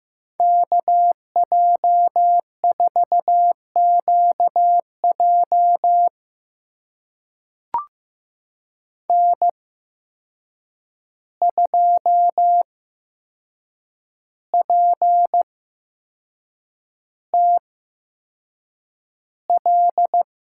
Morse code, heard as a keyed tone, about 15 wpm, spaced as for 6 wpm, 700 Hz. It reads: KJ4QJ N2PTL
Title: USA Call Signs - 5 Characters - 8x Character Spacing - 15wpm